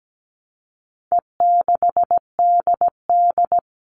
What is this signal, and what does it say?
Morse code, heard as a keyed tone, about 17 wpm, 700 Hz. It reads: E6DD
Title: All Cut Numbers - 4-Digit 17wpm